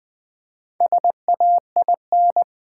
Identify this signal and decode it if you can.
Morse code, heard as a keyed tone, about 20 wpm, 700 Hz. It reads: SAIN
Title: Build A Word - 1 Letter At A Time - 5-Letter Words - 20wpm